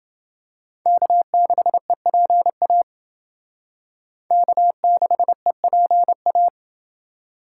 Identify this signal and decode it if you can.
Morse code, heard as a keyed tone, about 28 wpm, 700 Hz. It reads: K6EPA K6EPA